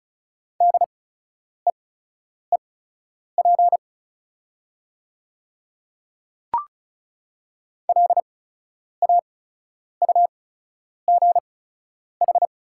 Morse code, heard as a keyed tone, about 36 wpm, spaced as for 10 wpm, 700 Hz. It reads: DEEP LAUGH
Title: 8x Character Spacing - Top 500 Words - 35wpm